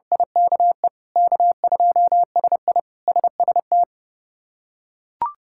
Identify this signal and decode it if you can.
Morse code, heard as a keyed tone, about 32 wpm, 700 Hz. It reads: IKE K2SI SST
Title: K1USN SST QSO Practice - 30wpm